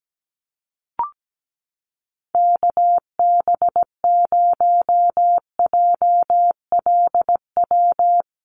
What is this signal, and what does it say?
Morse code, heard as a keyed tone, about 17 wpm, 700 Hz. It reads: KB0JLW
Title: USA Call Signs Repeated 2x 17wpm